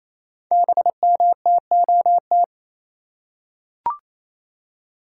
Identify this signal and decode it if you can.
Morse code, heard as a keyed tone, about 29 wpm, 700 Hz. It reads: BMTOT
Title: Code Groups - 5 Letters - 28wpm